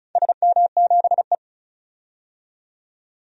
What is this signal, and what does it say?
Morse code, heard as a keyed tone, about 36 wpm, 700 Hz. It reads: SM7E
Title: DXCC-100 Call Signs - 4 Characters - 35wpm